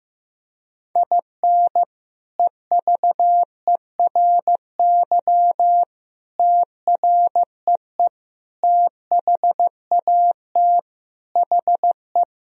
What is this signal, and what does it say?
Morse code, heard as a keyed tone, about 15 wpm, 700 Hz. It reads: IN EVERY TREE THAT HE
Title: Sentences from Top 1000 Words - Review Part 1 - 15wpm